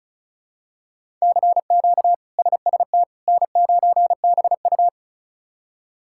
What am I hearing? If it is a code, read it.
Morse code, heard as a keyed tone, about 34 wpm, 700 Hz. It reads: CQ SST N9BU
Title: K1USN SST QSO Practice - 35wpm